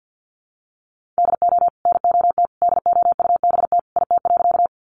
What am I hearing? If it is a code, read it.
Morse code, heard as a keyed tone, about 50 wpm, 700 Hz. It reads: DO NOT DOUBT IT.
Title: Sentences from Top 1500 Words 50wpm